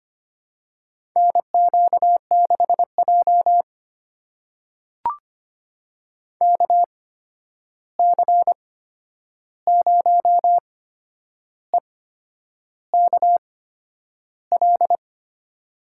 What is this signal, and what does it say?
Morse code, heard as a keyed tone, about 25 wpm, spaced as for 9 wpm, 700 Hz. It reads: NQ6J KC0EKL
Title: USA Call Signs - 8x Character Spacing - 25wpm